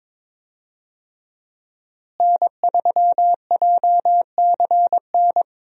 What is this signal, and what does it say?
Morse code, heard as a keyed tone, about 22 wpm, 700 Hz. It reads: N3JCN
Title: USA Call Signs - 5 Characters - Repeated 2x - 22wpm